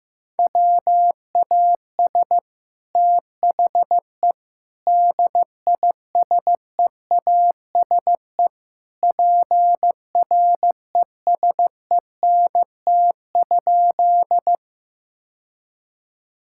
Morse code, heard as a keyed tone, about 15 wpm, 700 Hz. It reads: WAS THE DISEASE PRESENT?
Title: Sentences from Top 1000 Words - Review Part 6 - 15wpm